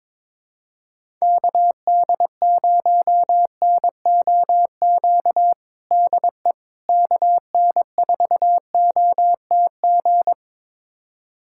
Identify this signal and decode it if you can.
Morse code, heard as a keyed tone, about 21 wpm, 700 Hz. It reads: KD0NOQ DE KN4OTG
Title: USA Callsign DE Callsign - 6 Characters - 22wpm